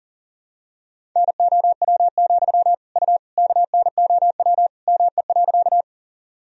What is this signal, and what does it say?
Morse code, heard as a keyed tone, about 37 wpm, 700 Hz. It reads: NOW, U KNOW ME.
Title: Sentences from Top 200 Words and Abbreviations 40wpm